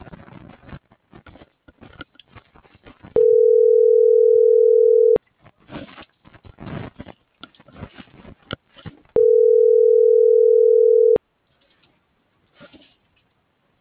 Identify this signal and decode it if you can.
Ringback tone (North America), 440 + 480 Hz.